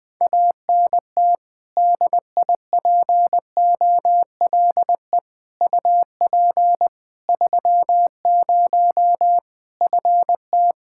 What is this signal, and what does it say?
Morse code, heard as a keyed tone, about 20 wpm, 700 Hz. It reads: ANT DIPOLE UP 30 FT
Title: K1USN SST QSO Practice 20wpm